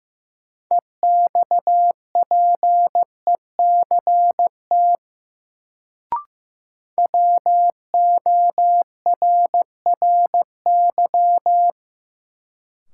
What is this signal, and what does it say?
Morse code, heard as a keyed tone, about 15 wpm, 700 Hz. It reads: EXPECT WORRY
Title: Sentences from Top 1100 Words - New Word Drill - 15wpm